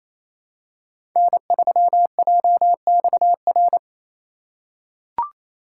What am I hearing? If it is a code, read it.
Morse code, heard as a keyed tone, about 27 wpm, 700 Hz. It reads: N3JXR
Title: USA Call Signs - 3x Character Spacing - 28wpm